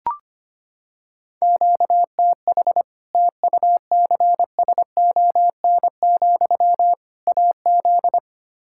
Morse code, heard as a keyed tone, about 25 wpm, 700 Hz. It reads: QTH TUCSON, AZ
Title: Practice QSO - Exchange 1 - Beginning - 25wpm